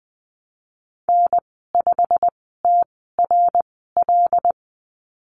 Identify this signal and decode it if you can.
Morse code, heard as a keyed tone, about 20 wpm, spaced as for 15 wpm, 700 Hz. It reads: N5TRL
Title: 122 Lesson 20g - Number 3 - Call Signs 20wpm